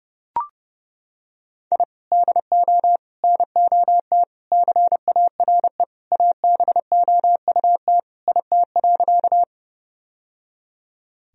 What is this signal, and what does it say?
Morse code, heard as a keyed tone, about 32 wpm, 700 Hz. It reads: I DO NOT CARE ABOUT IT.
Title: Sentences from Top 1000 Words - Review Part 1 - 30wpm